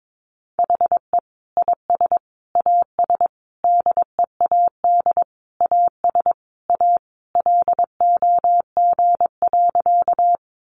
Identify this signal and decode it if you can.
Morse code, heard as a keyed tone, about 22 wpm, 700 Hz. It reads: HE IS AS DEAD AS A LOG.
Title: Sentences from Top 1500 Words 22wpm